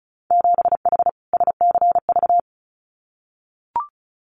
Morse code, heard as a keyed tone, about 34 wpm, 700 Hz. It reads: 7H SCV
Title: Winter Field Day Exchanges Encore 35wpm